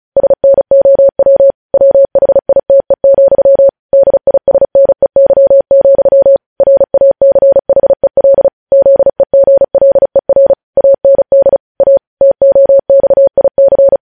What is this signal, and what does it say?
Morse code, heard as a keyed tone, about 36 wpm, 550 Hz. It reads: SNOW WHITE, DISNEY, RACHEL ZEGLER AND A TOXIC
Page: News headlines in Morse code at 35 wpm